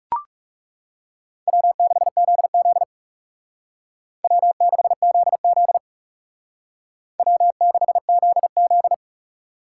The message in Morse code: W6ZZ W6ZZ W6ZZ

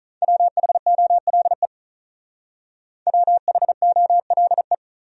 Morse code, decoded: WHOLE WHOLE

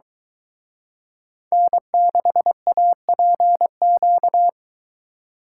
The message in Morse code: N6APQ

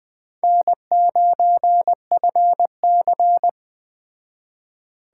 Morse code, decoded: N9FC

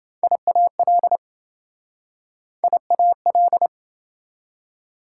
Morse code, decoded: IAL IAL

